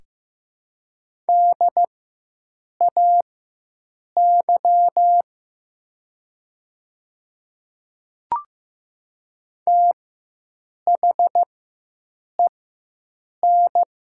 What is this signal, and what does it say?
Morse code, heard as a keyed tone, about 15 wpm, spaced as for 7 wpm, 700 Hz. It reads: DAY THEN
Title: 4x Character Spacing - Top 100 Words - 15wpm